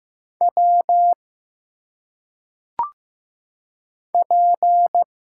Morse code, heard as a keyed tone, about 15 wpm, 700 Hz. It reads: W P
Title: Single Letters - Phonetic Pronunciation 15wpm